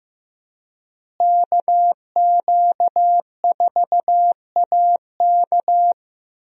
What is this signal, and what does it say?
Morse code, heard as a keyed tone, about 15 wpm, 700 Hz. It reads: KQ4AK
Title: USA Call Signs - 5 Characters - Repeated 2x - 15wpm